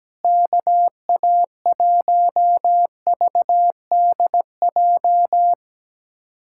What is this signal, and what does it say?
Morse code, heard as a keyed tone, about 17 wpm, 700 Hz. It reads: KA1VDJ
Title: USA Callsign DE Callsign - 6 Characters - 17wpm